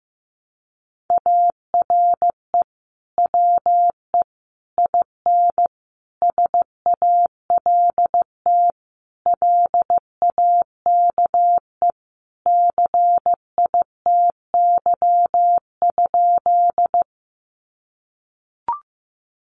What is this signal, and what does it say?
Morse code, heard as a keyed tone, about 15 wpm, 700 Hz. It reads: ARE WE IN SALT LAKE CITY?
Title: Sentences from Top 1100 Words 15wpm